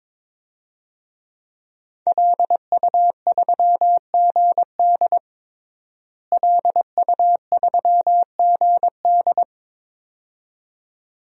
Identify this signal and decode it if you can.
Morse code, heard as a keyed tone, about 22 wpm, 700 Hz. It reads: LU3GD LU3GD
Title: DXCC-100 Call Signs - 5 Characters - Repeated 2x - 22wpm